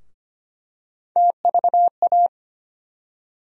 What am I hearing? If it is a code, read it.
Morse code, heard as a keyed tone, about 25 wpm, 700 Hz. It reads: TVA
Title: USA Call Sign Postfixes - 3 Letters 25wpm